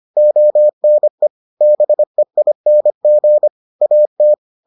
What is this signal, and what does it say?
Morse code, heard as a keyed tone, about 25 wpm, 600 Hz. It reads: ONE BEING AT